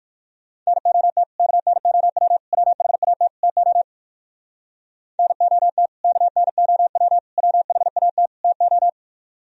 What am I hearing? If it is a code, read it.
Morse code, heard as a keyed tone, about 58 wpm, 700 Hz. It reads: NOT KNOW WHAT TO NOT KNOW WHAT TO